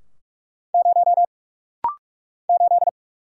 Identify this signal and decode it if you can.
Morse code, heard as a keyed tone, about 44 wpm, 700 Hz. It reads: 0 8
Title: Single Letter-Number - Rapid-Fire 45wpm